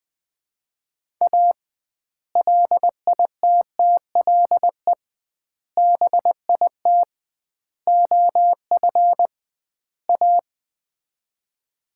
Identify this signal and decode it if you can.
Morse code, heard as a keyed tone, about 19 wpm, 700 Hz. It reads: A LITTLE BIT OF A
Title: 2x Word-Spacing - Sets of 5 Words - 20wpm